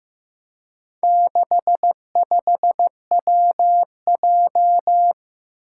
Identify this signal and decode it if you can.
Morse code, heard as a keyed tone, about 15 wpm, 700 Hz. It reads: 65WJ
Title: Four Challenging Letter-Number Combinations 15wpm